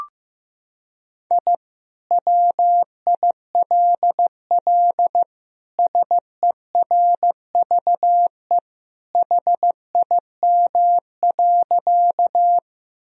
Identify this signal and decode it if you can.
Morse code, heard as a keyed tone, about 15 wpm, 700 Hz. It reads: I WILL SERVE HIM.
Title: Sentences from Top 1000 Words - Review Part 2 - 15wpm